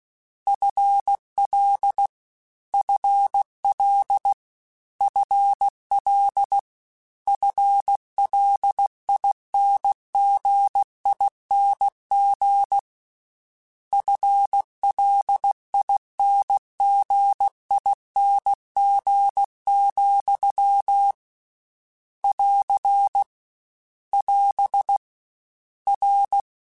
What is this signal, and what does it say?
Morse code, heard as a keyed tone, about 16 wpm, 800 Hz. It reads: FL FL FL FLINGING FLINGING, <AR> <AS> R
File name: Combinations / Combinations-mp3s / Combinations@15